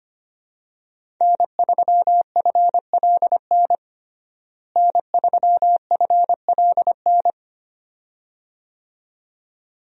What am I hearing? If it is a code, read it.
Morse code, heard as a keyed tone, about 25 wpm, 700 Hz. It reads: N3FLN N3FLN
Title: USA Call Signs - 5 Characters - Repeated 2x - 25wpm